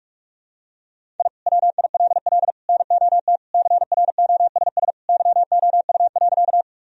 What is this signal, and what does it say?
Morse code, heard as a keyed tone, about 41 wpm, 700 Hz. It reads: I WILL NOT CROSS YOU.
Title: Sentences from Top 1000 Words - Review Part 1 - 45wpm